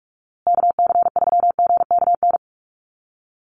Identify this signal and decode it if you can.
Morse code, heard as a keyed tone, about 44 wpm, 700 Hz. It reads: KC3GKN
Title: USA Call Signs Repeated 2x 45wpm